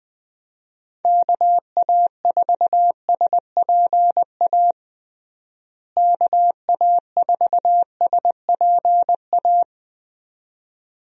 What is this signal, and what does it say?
Morse code, heard as a keyed tone, about 20 wpm, 700 Hz. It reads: KA4SPA KA4SPA
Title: USA Call Signs - 6 Characters - Repeated 2x - 20wpm